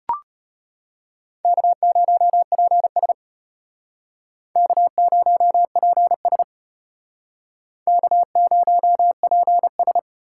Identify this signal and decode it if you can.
Morse code, heard as a keyed tone, about 32 wpm, 700 Hz. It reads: K0PS K0PS K0PS